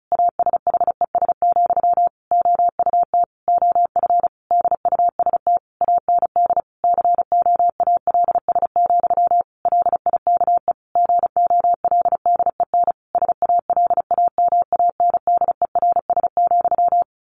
Morse code, heard as a keyed tone, about 34 wpm, 700 Hz. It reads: ASHES, OUT OF DUST AND COALS, LIKE GOLDEN SALAMANDERS,
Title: A Sound Of Thunder 35wpm